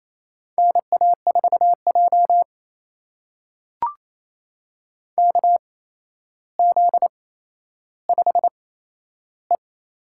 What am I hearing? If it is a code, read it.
Morse code, heard as a keyed tone, about 28 wpm, spaced as for 12 wpm, 700 Hz. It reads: NA4J KZ5E
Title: USA Call Signs - 4 Characters - 8x Character Spacing - 28wpm